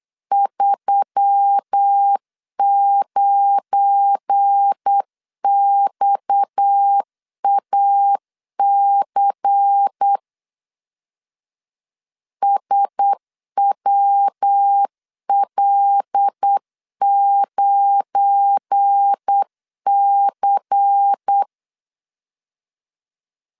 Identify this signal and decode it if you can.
Morse code, heard as a keyed tone, about 8 wpm, 800 Hz. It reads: 39XAC SWL9C
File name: LEZIONI / 7 GF9Z / 7 GF9 8wpm 3545